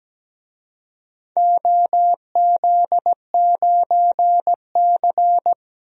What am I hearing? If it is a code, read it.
Morse code, heard as a keyed tone, about 17 wpm, 700 Hz. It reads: OZ9C
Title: DXCC-100 Call Signs - 4 Characters - Repeated 2x - 17wpm